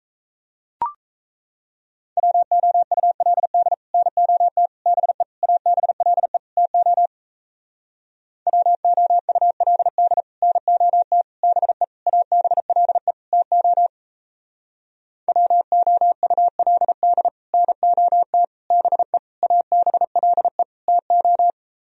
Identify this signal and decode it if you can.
Morse code, heard as a keyed tone, about 37 wpm, 700 Hz. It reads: WOULD NOT BE ABLE TO WOULD NOT BE ABLE TO WOULD NOT BE ABLE TO